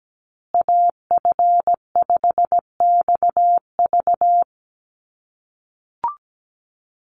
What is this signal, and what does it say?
Morse code, heard as a keyed tone, about 17 wpm, 700 Hz. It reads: AF5XV
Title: USA Call Signs Repeated 2x 17wpm